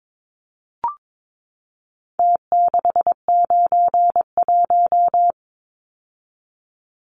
Morse code, heard as a keyed tone, about 22 wpm, 700 Hz. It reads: T691
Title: Numbers - Contest Style - 4-Digit 22wpm